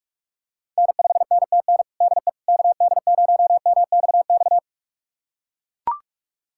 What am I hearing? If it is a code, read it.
Morse code, heard as a keyed tone, about 43 wpm, 700 Hz. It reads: N5NTN DE KD0MXX
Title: USA Callsign DE Callsign 45wpm